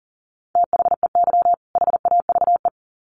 Morse code, heard as a keyed tone, about 40 wpm, 700 Hz. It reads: THEY HAVE
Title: Sets of 2 Words 40wpm